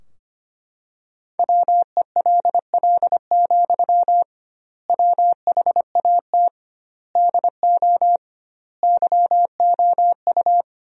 Morse code, heard as a keyed tone, about 25 wpm, 700 Hz. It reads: WELL, WHAT DO YOU
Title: 2x Word-Spacing - Sentences from Top 200 Words - 25wpm